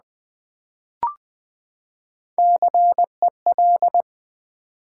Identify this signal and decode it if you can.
Morse code, heard as a keyed tone, about 20 wpm, 700 Hz. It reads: CEL
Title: Build A Word - 3 Letters At A Time - 9-Letter Words - 20wpm